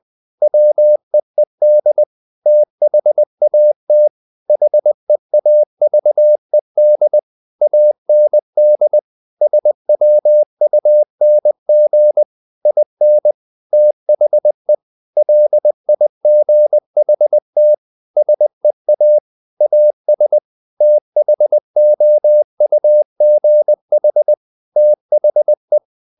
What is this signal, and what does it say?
Morse code, heard as a keyed tone, about 20 wpm, 600 Hz. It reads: WEED THAT HEAVED AND SWUNG IN THE LIGHT SEA AS THOUGH THE
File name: the old man and the sea part 2 20wpm